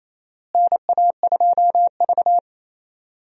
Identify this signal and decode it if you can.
Morse code, heard as a keyed tone, about 27 wpm, 700 Hz. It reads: NA2V
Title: USA Call Signs - 4 Characters - Repeated 3x - 28wpm